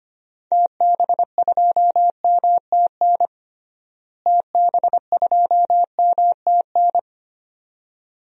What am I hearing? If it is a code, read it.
Morse code, heard as a keyed tone, about 25 wpm, 700 Hz. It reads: TB2MTN TB2MTN
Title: DXCC-100 Call Signs - 6 Characters - Repeated 2x - 25wpm